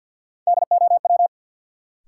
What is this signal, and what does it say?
Morse code, heard as a keyed tone, about 52 wpm, 700 Hz. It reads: DOW
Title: Sets of 5 Words - New Word Drill - 50wpm